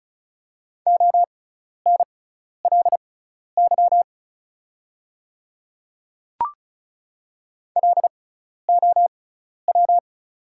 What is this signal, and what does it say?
Morse code, heard as a keyed tone, about 34 wpm, spaced as for 12 wpm, 700 Hz. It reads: ONLY LOW
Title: 6x Character Spacing - Top 200 Words - 35wpm